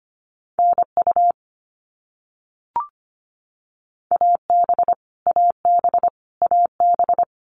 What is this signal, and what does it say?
Morse code, heard as a keyed tone, about 25 wpm, 700 Hz. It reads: NU AB AB AB
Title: Canadian Provinces and Territories - Repeated 3x - 25wpm